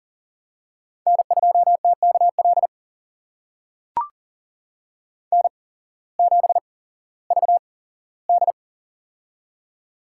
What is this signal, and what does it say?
Morse code, heard as a keyed tone, about 42 wpm, spaced as for 19 wpm, 700 Hz. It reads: N2TKL N7VD
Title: USA Call Signs - 8x Character Spacing - 40wpm